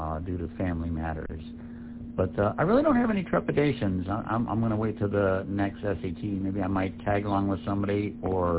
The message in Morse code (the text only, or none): none